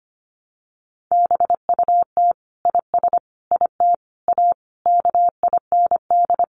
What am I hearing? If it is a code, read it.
Morse code, heard as a keyed tone, about 25 wpm, 700 Hz. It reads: BUT IS IT A KIND
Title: Sentences from Top 1200 Words 25wpm